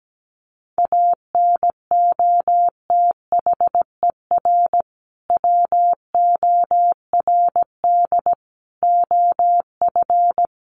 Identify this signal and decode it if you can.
Morse code, heard as a keyed tone, about 17 wpm, 700 Hz. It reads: ANOTHER WORD OF
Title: Sentences from Top 1100 Words 17wpm